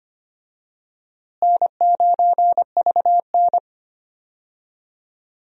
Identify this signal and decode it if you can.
Morse code, heard as a keyed tone, about 25 wpm, 700 Hz. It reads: N9VN
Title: USA Call Signs - 4 Characters - Repeated 3x - 25wpm